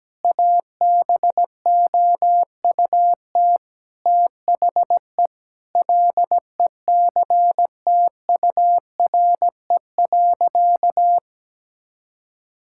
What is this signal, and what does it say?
Morse code, heard as a keyed tone, about 17 wpm, 700 Hz. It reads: ABOUT THE LECTURE.
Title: Sentences from Top 1600 Words 17wpm